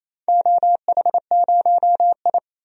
Morse code, heard as a keyed tone, about 28 wpm, 700 Hz. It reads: OH0I
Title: DXCC-100 Call Signs - 4 Characters - 28wpm